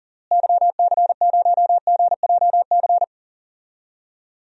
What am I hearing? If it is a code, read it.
Morse code, heard as a keyed tone, about 40 wpm, 700 Hz. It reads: YC0GJC
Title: DXCC-100 Call Signs - 6 Characters - Repeated 3x - 40wpm